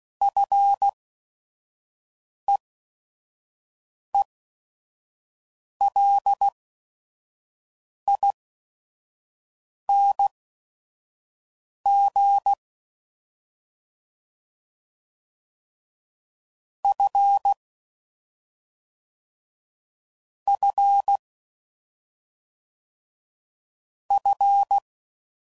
Morse code, heard as a keyed tone, about 16 wpm, spaced as for 5 wpm, 800 Hz. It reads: FEELING F F F